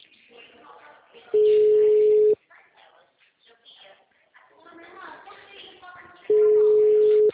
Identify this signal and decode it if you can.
Ringback tone (Europe, 425 Hz), with speech.